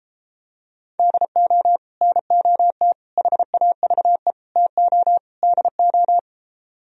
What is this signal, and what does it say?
Morse code, heard as a keyed tone, about 33 wpm, 700 Hz. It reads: DO NOT HAVE TO DO